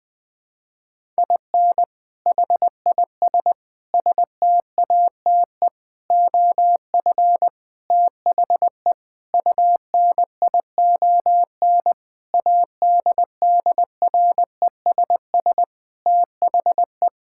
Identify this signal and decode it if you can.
Morse code, heard as a keyed tone, about 21 wpm, 700 Hz. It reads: IN HIS STATE OF THE UNION ADDRESS THE
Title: Sentences with 7-Word Phrases - 20wpm